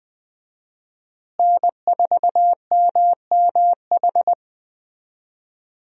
Morse code, heard as a keyed tone, about 20 wpm, 700 Hz. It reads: N4MMH
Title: USA Call Signs - 5 Characters - Repeated 3x - 20wpm